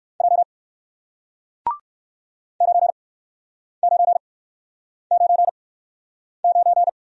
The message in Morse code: J 9 9 9 9

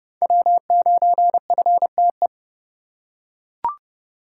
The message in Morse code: W9FTE

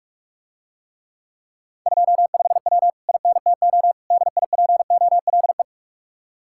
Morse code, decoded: 25W INTO DIPOLE

